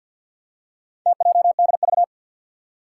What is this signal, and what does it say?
Morse code, heard as a keyed tone, about 48 wpm, 700 Hz. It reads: TJDV
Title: Four Challenging Letter-Number Combinations 50wpm